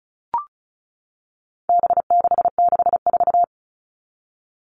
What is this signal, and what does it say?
Morse code, heard as a keyed tone, about 34 wpm, 700 Hz. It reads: B664
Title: All Cut Numbers - 4-Digit 35wpm